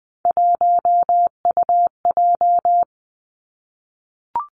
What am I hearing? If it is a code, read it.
Morse code, heard as a keyed tone, about 20 wpm, 700 Hz. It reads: 1UJ
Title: Three Challenging Letter-Number Combinations 20wpm